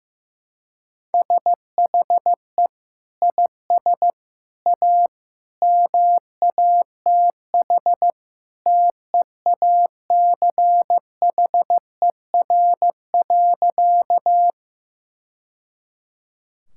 Morse code, heard as a keyed tone, about 16 wpm, 700 Hz. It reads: SHE IS A MATH TEACHER.
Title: Sentences from Top 1000 Words - Review Part 6 - 15wpm